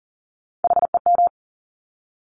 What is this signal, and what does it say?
Morse code, heard as a keyed tone, about 38 wpm, 700 Hz. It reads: HEM
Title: Top Three Letter Combinations in Words 40wpm